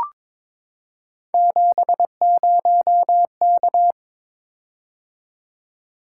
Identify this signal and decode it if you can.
Morse code, heard as a keyed tone, about 22 wpm, 700 Hz. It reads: 70K
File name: practice / 22wpm / Three Letter-Number Combinations - 22wpm